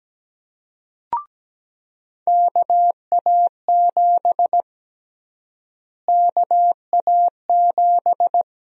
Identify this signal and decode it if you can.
Morse code, heard as a keyed tone, about 17 wpm, 700 Hz. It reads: KA7 KA7
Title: USA Call Sign Prefixes - Repeated 2x - 17wpm